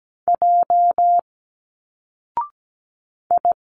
Morse code, heard as a keyed tone, about 17 wpm, 700 Hz. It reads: J I
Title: Single Letters 17wpm